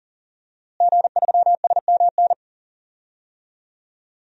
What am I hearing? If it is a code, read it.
Morse code, heard as a keyed tone, about 40 wpm, 700 Hz. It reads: G3SMN
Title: DXCC-100 Call Signs - 5 Characters - Repeated 3x - 40wpm